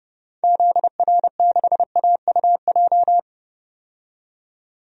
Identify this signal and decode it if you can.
Morse code, heard as a keyed tone, about 30 wpm, 700 Hz. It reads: ZR6AUJ
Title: DXCC-100 Call Signs - 6 Characters - 30wpm